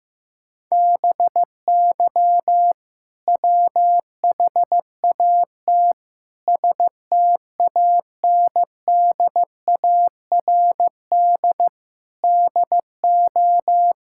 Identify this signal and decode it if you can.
Morse code, heard as a keyed tone, about 15 wpm, 700 Hz. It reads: BY WHAT STANDARD DO
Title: Sentences from Top 1000 Words - Review Part 6 - 15wpm